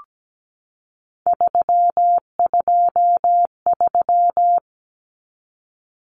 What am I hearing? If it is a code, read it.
Morse code, heard as a keyed tone, about 17 wpm, 700 Hz. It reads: 323
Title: Common Cut Numbers - 3-Digit 17wpm